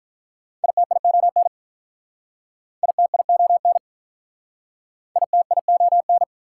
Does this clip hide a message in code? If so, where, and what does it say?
Yes: Morse code from 0.6 to 6.2 s, heard as a keyed tone, about 44 wpm, 700 Hz. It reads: ITION ITION ITION